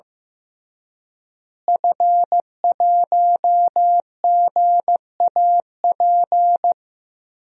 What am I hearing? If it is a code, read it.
Morse code, heard as a keyed tone, about 15 wpm, 700 Hz. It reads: F1GAP
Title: DXCC-100 Call Signs - 5 Characters - Repeated 3x - 15wpm